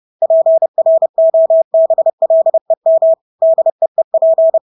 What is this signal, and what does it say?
Morse code, heard as a keyed tone, about 30 wpm, 650 Hz. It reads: PROBLEM DEEP
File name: Engelske ord 30-30 wpm